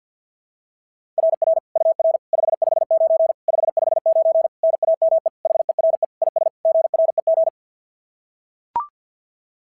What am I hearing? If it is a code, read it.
Morse code, heard as a keyed tone, about 45 wpm, 660 Hz. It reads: RR UR 559 559 NAME HERE IS GREG
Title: SOTA QSO Practice 50wpm